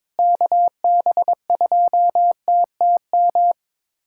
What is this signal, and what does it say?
Morse code, heard as a keyed tone, about 22 wpm, 700 Hz. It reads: KB2TTM
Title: USA Callsign DE Callsign - 6 Characters - 22wpm